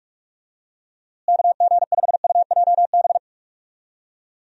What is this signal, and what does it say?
Morse code, heard as a keyed tone, about 47 wpm, 700 Hz. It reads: KG5UJB
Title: USA Call Signs - 6 Characters - Repeated 3x - 45wpm